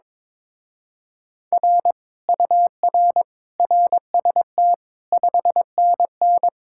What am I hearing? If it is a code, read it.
Morse code, heard as a keyed tone, about 22 wpm, 700 Hz. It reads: R UR RST 5NN
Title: Speed-Racing Common QSO Phrases Encore - 22wpm